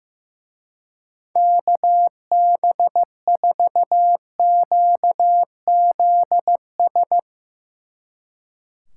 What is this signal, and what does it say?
Morse code, heard as a keyed tone, about 15 wpm, 700 Hz. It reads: KB4QZS
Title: Speed Racing - USA 2x3 Call Signs 15wpm